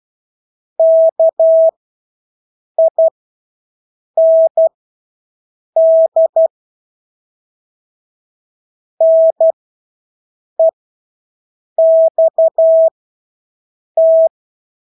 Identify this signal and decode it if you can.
Morse code, heard as a keyed tone, about 12 wpm, spaced as for 6 wpm, 650 Hz. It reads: KIND NEXT
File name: Engelske ord 12-6 wpm